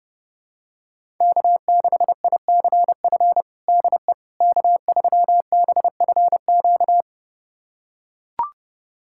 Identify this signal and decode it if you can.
Morse code, heard as a keyed tone, about 30 wpm, 700 Hz. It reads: K6ICF DE K3BFQ
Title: USA Callsign DE Callsign - 5 Characters - 30wpm